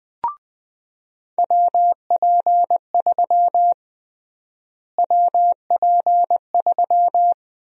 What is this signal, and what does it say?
Morse code, heard as a keyed tone, about 20 wpm, 700 Hz. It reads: WP3 WP3
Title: USA Call Sign Prefixes - Repeated 2x - 20wpm